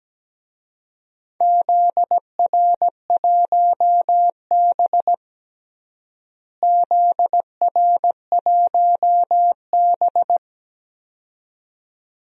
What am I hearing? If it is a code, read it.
Morse code, heard as a keyed tone, about 17 wpm, 700 Hz. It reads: ZR1B ZR1B
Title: DXCC-100 Call Signs - 4 Characters - Repeated 2x - 17wpm